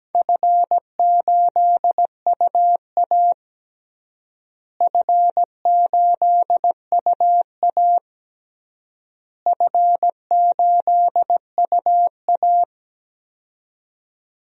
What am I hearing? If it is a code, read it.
Morse code, heard as a keyed tone, about 17 wpm, 700 Hz. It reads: F8UA F8UA F8UA
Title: DXCC-100 Call Signs - 4 Characters - Repeated 3x - 17wpm